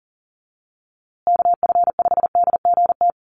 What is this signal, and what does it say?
Morse code, heard as a keyed tone, about 40 wpm, 700 Hz. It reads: KF5DGT
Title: USA Call Signs Repeated 3x 40wpm